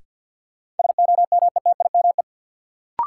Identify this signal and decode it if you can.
Morse code, heard as a keyed tone, about 47 wpm, 700 Hz. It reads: SOMETIME
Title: Sentences from Top 1000 Words - Review Part 6 - New Word Drill - 50wpm